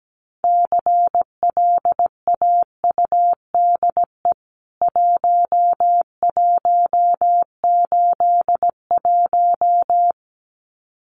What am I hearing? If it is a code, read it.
Morse code, heard as a keyed tone, about 17 wpm, 700 Hz. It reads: CLAUDE 1181
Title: CWT Exchange 17wpm